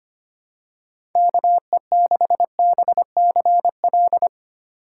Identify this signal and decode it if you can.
Morse code, heard as a keyed tone, about 25 wpm, 700 Hz. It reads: KE6BCL
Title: USA Call Signs - 6 Characters - 4x Character Spacing - 25wpm